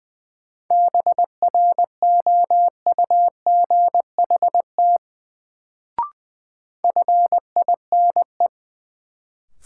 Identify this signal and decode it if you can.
Morse code, heard as a keyed tone, about 20 wpm, 700 Hz. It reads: BROUGHT FINE
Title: Sentences from Top 500 Words - New Word Drill - 20wpm